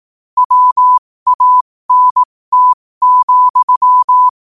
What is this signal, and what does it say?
Morse code, heard as a keyed tone, about 18 wpm, spaced as for 15 wpm, 1000 Hz. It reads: WANT,
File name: qotdfiles / 15wpm